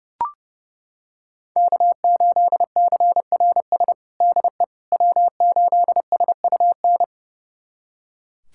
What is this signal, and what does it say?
Morse code, heard as a keyed tone, about 30 wpm, 700 Hz. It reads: K8CRS DE W8SUN
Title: USA Callsign DE Callsign - 5 Characters - 30wpm